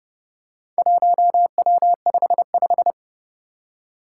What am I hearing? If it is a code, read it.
Morse code, heard as a keyed tone, about 30 wpm, 700 Hz. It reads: 1W55
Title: Four Challenging Letter-Number Combinations 30wpm